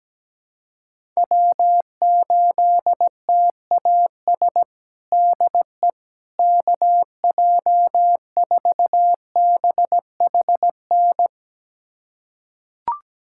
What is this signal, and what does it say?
Morse code, heard as a keyed tone, about 17 wpm, 700 Hz. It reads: W8TAS DE KJ4BHN
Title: USA Callsign DE Callsign 17wpm